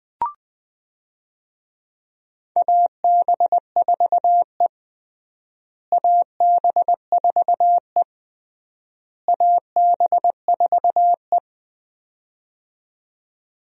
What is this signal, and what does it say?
Morse code, heard as a keyed tone, about 20 wpm, 700 Hz. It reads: AB4E AB4E AB4E
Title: USA Call Signs - 4 Characters - Repeated 3x - 20wpm